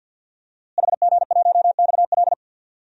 Morse code, heard as a keyed tone, about 53 wpm, 700 Hz. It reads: HG1XL
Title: DXCC-100 Call Signs - 5 Characters - Repeated 2x - 50wpm